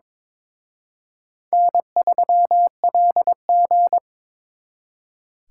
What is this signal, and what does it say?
Morse code, heard as a keyed tone, about 26 wpm, 700 Hz. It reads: N3LG
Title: Speed Racing - USA 1x2 Call Signs 22wpm